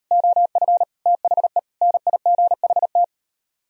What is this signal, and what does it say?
Morse code, heard as a keyed tone, about 38 wpm, 700 Hz. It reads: OF THE NIGHT